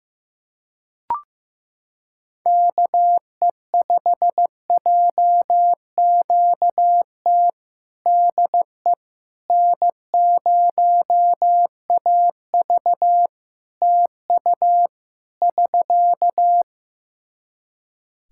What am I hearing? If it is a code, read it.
Morse code, heard as a keyed tone, about 15 wpm, 700 Hz. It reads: KE5JQT DE N0AV TU <SK>
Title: Practice QSO - Exchange 4 - Ending - 15wpm